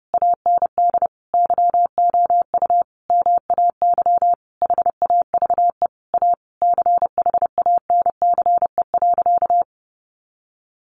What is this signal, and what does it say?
Morse code, heard as a keyed tone, about 28 wpm, 700 Hz. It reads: AND YOU MAY HAVE A CHANCE.
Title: Sentences from Top 1400 Words 30wpm